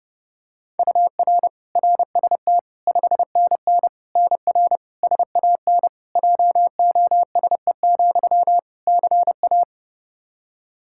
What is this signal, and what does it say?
Morse code, heard as a keyed tone, about 30 wpm, 700 Hz. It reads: UR RST 5NN NR SAN JOSE, CA
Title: Practice QSO - Exchange 1 - Beginning - 30wpm